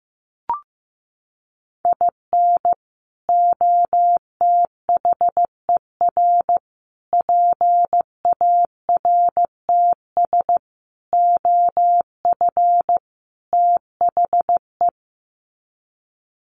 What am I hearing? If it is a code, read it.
Morse code, heard as a keyed tone, about 15 wpm, 700 Hz. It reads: IN OTHER PARTS OF THE
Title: Sets of 5 Words 15wpm